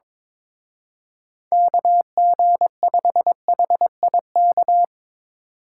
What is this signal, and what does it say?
Morse code, heard as a keyed tone, about 24 wpm, 700 Hz. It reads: KG5HIK
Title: Speed Racing - USA Call Signs 22wpm